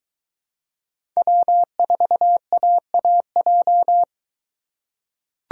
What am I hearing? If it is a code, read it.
Morse code, heard as a keyed tone, about 22 wpm, 700 Hz. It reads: W4AAJ